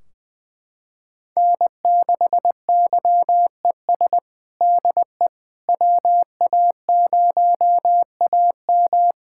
Morse code, heard as a keyed tone, about 21 wpm, 700 Hz. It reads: N6YES DE WA0AM
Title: Practice QSO - Exchange 4 - Ending - 20wpm